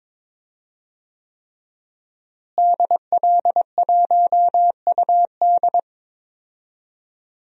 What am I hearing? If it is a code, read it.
Morse code, heard as a keyed tone, about 22 wpm, 700 Hz. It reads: DL1UD
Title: DXCC-100 Call Signs - 5 Characters - Repeated 3x - 22wpm